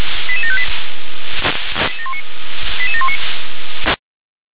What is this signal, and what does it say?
Five-tone selective call (ZVEI), heard as quick runs of five tones, tones of about 70 ms. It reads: 88758 88719